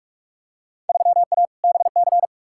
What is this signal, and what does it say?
Morse code, heard as a keyed tone, about 43 wpm, 700 Hz. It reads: 3A BC
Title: Field Day Exchanges - 45wpm